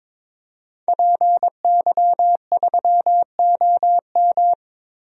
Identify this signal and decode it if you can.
Morse code, heard as a keyed tone, about 22 wpm, 700 Hz. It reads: PY3OM
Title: DXCC-100 Call Signs - 5 Characters - 22wpm